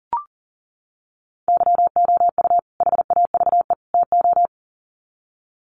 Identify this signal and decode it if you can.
Morse code, heard as a keyed tone, about 40 wpm, 700 Hz. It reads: YOU HAVE TO
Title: Sets of 3 Words 40wpm